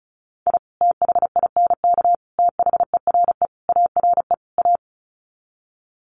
Morse code, heard as a keyed tone, about 34 wpm, 700 Hz. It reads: I THINK THERE ARE A
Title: Sets of 5 Words Encore2 35wpm